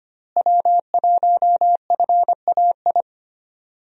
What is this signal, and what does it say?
Morse code, heard as a keyed tone, about 25 wpm, 700 Hz. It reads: W1FAI
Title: USA Call Signs - 5 Characters - Repeated 2x - 25wpm